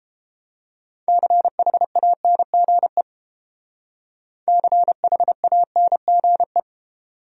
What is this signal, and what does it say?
Morse code, heard as a keyed tone, about 29 wpm, 700 Hz. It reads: CHANGE CHANGE